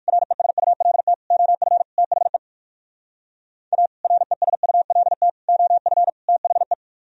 Morse code, heard as a keyed tone, about 61 wpm, 700 Hz. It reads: RESULT OF THE A RESULT OF THE